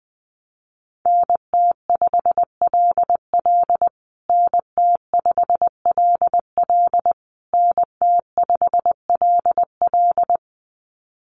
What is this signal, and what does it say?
Morse code, heard as a keyed tone, about 20 wpm, 700 Hz. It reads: NT5LL NT5LL NT5LL
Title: USA Call Signs Repeated 3x 20wpm